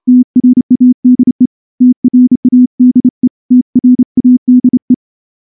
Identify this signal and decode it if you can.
Morse code, heard as a keyed tone, about 28 wpm, 250 Hz. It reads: TRADE TRADE TRADE
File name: morse-training-30WPM